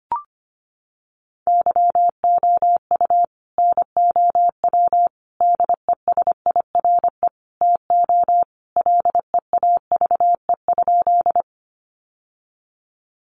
Morse code, heard as a keyed tone, about 25 wpm, 700 Hz. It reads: YOU NOW DESIRE TO LEAVE?
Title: Sentences from Top 1100 Words 25wpm